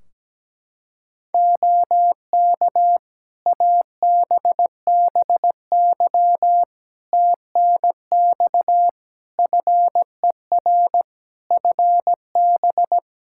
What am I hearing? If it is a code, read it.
Morse code, heard as a keyed tone, about 17 wpm, 700 Hz. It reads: OK ABBY TNX FER FB
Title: Practice QSO - Exchange 4 - Ending - 17wpm